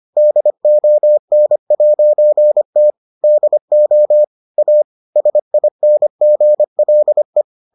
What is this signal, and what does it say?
Morse code, heard as a keyed tone, about 25 wpm, 600 Hz. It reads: DON'T DO A SINGLE